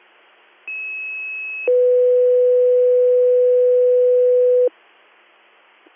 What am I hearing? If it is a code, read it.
Two-tone page, heard as a one-second tone, then a three-second tone, about 2600 then 500 Hz. It is heard from 0.7 to 4.7 s.